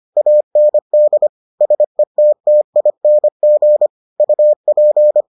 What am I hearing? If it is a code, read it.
Morse code, heard as a keyed tone, about 25 wpm, 600 Hz. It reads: AND SETTING UP